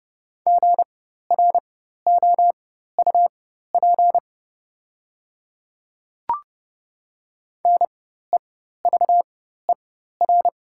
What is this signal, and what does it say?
Morse code, heard as a keyed tone, about 30 wpm, spaced as for 14 wpm, 690 Hz. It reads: GROUP NEVER
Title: 4x Character Spacing - Top 300 Words - 30wpm